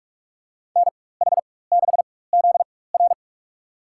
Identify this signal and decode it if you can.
Morse code, heard as a keyed tone, about 46 wpm, spaced as for 21 wpm, 700 Hz. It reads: NH6ZR
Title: USA Call Signs - 5 Characters - 4x Character Spacing - 45wpm